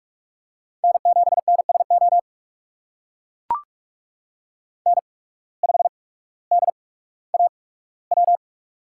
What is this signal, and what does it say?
Morse code, heard as a keyed tone, about 41 wpm, spaced as for 19 wpm, 700 Hz. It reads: N7NSO N5DAW
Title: USA Call Signs - 5 Characters - 8x Character Spacing - 45wpm